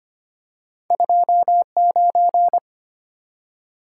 Morse code, heard as a keyed tone, about 26 wpm, 700 Hz. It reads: 29
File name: Two Letter-Number Combinations - 25wpm